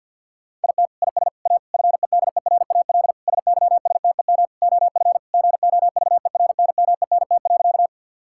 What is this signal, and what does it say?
Morse code, heard as a keyed tone, about 50 wpm, 700 Hz. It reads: IT IS A FEDERAL SYSTEM OF GOVERNMENT.
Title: Sentences from Top 1000 Words - Review Part 5 - 50wpm